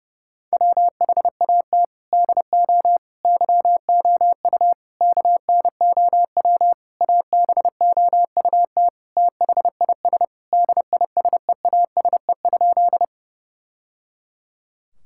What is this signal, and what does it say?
Morse code, heard as a keyed tone, about 30 wpm, 700 Hz. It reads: WHAT DO YOU KNOW ABOUT THIS DISEASE?
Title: Sentences from Top 1000 Words - Review Part 6 - 30wpm